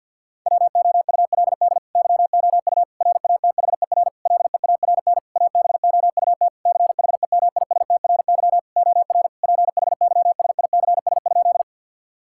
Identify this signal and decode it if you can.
Morse code, heard as a keyed tone, about 50 wpm, 700 Hz. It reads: WOULD YOU RATHER LEARN ABOUT CHEMISTRY OR PHYSICS?
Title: Sentences from Top 1000 Words - Review Part 6 - 50wpm